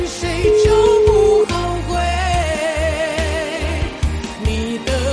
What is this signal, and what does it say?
Ringback tone (Europe, 425 Hz), with noise.